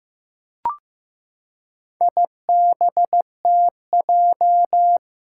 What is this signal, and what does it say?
Morse code, heard as a keyed tone, about 15 wpm, 700 Hz. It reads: IBTJ
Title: Four Challenging Letter-Number Combinations 15wpm